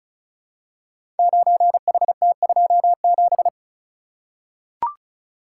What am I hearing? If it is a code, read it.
Morse code, heard as a keyed tone, about 36 wpm, 690 Hz. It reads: 9HT27
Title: Code Groups - 5 Letter:Numbers - 35wpm